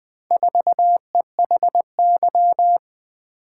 Morse code, heard as a keyed tone, about 20 wpm, 700 Hz. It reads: 4EHY
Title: Four Letter-Number Combinations - 20wpm